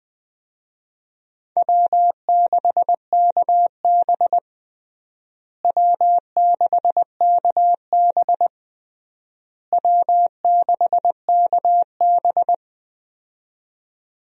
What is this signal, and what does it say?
Morse code, heard as a keyed tone, about 20 wpm, 700 Hz. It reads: W6KB W6KB W6KB
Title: USA Call Signs - 4 Characters - Repeated 3x - 20wpm